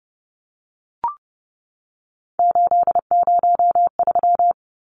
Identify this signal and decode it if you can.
Morse code, heard as a keyed tone, about 30 wpm, 700 Hz. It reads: 803
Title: Numbers - 3-Digit 30wpm